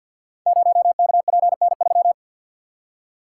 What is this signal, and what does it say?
Morse code, heard as a keyed tone, about 53 wpm, 700 Hz. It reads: 0KPN3
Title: Code Groups - 5 Letter:Numbers - 50wpm